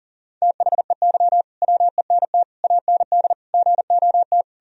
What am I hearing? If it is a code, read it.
Morse code, heard as a keyed tone, about 40 wpm, 700 Hz. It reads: THEY WENT AND GOT
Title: Sentences from Top 1000 Words - Review Part 3 - 40wpm